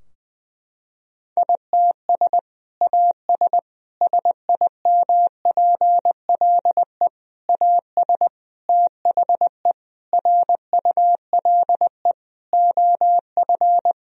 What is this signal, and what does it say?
Morse code, heard as a keyed tone, about 20 wpm, 700 Hz. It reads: ITS AS SIMPLE AS THE RULE OF
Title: Sentences from Top 1000 Words - Review Part 2 - 20wpm